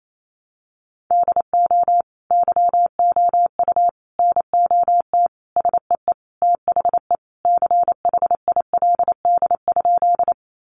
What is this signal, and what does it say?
Morse code, heard as a keyed tone, about 28 wpm, 700 Hz. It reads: DO YOU NOT SEE THE CHILD?
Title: Sentences from Top 800 Words 28wpm